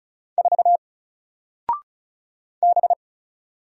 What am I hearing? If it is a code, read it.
Morse code, heard as a keyed tone, about 35 wpm, 700 Hz. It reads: 4 B
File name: Single-Letter-Number-35wpm